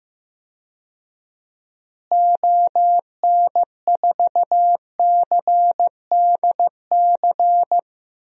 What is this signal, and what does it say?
Morse code, heard as a keyed tone, about 15 wpm, 700 Hz. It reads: ON4CDC
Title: DXCC-100 Call Signs - 6 Characters - Repeated 3x - 15wpm